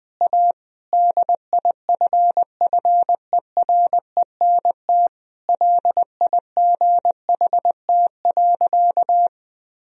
Morse code, heard as a keyed tone, about 20 wpm, 700 Hz. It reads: A DIFFERENT LIGHT.
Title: Sentences from Top 1000 Words - Review Part 3 - 20wpm